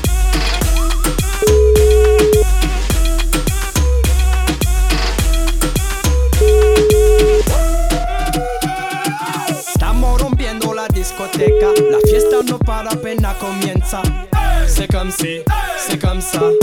Ringback tone (Europe, 425 Hz), with noise.